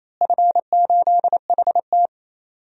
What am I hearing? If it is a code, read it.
Morse code, heard as a keyed tone, about 28 wpm, 700 Hz. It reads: F8HT
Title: DXCC-100 Call Signs - 4 Characters - Repeated 2x - 28wpm